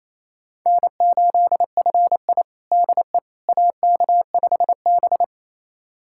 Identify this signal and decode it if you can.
Morse code, heard as a keyed tone, about 28 wpm, 700 Hz. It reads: N8FI DE AK5B
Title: USA Callsign DE Callsign - 4 Characters - 28wpm